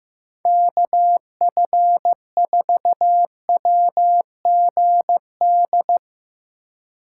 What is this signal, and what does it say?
Morse code, heard as a keyed tone, about 15 wpm, 700 Hz. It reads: KF4WGD
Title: Speed Racing - USA 2x3 Call Signs 15wpm